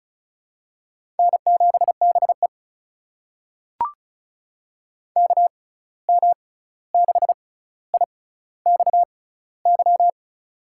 Morse code, heard as a keyed tone, about 36 wpm, spaced as for 16 wpm, 700 Hz. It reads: N7BE KM6IXY
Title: USA Call Signs - 6x Character Spacing - 35wpm